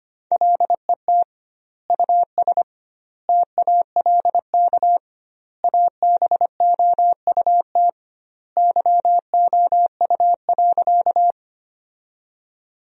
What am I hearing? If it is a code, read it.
Morse code, heard as a keyed tone, about 25 wpm, 700 Hz. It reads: LET US TALK ABOUT YOU.
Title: 2x Word-Spacing - Sentences from Top 400 Words - 25wpm